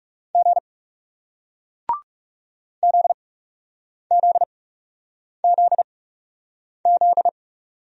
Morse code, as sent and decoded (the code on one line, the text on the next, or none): --. / --.. --.. --.. --..
G ZZZZ